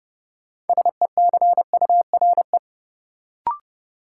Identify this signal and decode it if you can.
Morse code, heard as a keyed tone, about 30 wpm, 700 Hz. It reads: SECURE
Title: Sentences from Top 1200 Words - New Word Drill Alt - 30wpm